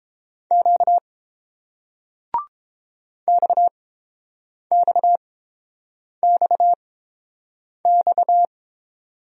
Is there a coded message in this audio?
Yes: Morse code from 0.5 to 8.5 s, heard as a keyed tone, about 29 wpm, spaced as for 7 wpm, 700 Hz. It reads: Q XXXX